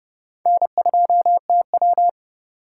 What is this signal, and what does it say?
Morse code, heard as a keyed tone, about 28 wpm, 700 Hz. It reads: N2TW